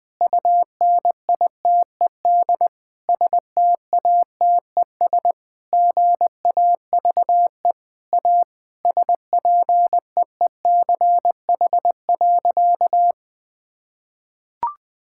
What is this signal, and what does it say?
Morse code, heard as a keyed tone, about 21 wpm, 700 Hz. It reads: UNITED STATES GAVE A SPEECH.
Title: Sentences with 5-Word Phrases - 20wpm